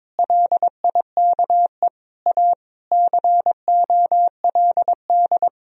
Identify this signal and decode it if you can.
Morse code, heard as a keyed tone, about 22 wpm, 700 Hz. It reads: LIKE A COLD
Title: Sentences from Top 1000 Words - Review Part 2 - 22wpm